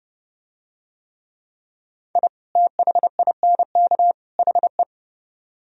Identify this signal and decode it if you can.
Morse code, heard as a keyed tone, about 30 wpm, 700 Hz. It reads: I THINK HE